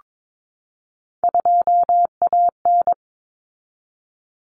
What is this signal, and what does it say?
Morse code, heard as a keyed tone, about 22 wpm, 700 Hz. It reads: 2AN
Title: Common Cut Numbers - 3-Digit 22wpm